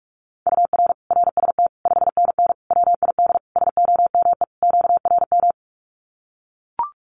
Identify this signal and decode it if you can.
Morse code, heard as a keyed tone, about 44 wpm, 700 Hz. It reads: UR RST 5NN WID SOME QRM
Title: Common QSO Phrases 45wpm